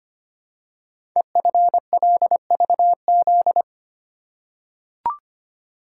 Morse code, heard as a keyed tone, about 25 wpm, 700 Hz. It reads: EFLVZ
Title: Code Groups - 5 Letters - 25wpm